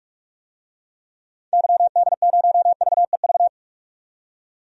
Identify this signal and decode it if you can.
Morse code, heard as a keyed tone, about 46 wpm, 700 Hz. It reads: YD0VEV
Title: DXCC-100 Call Signs - 6 Characters - 45wpm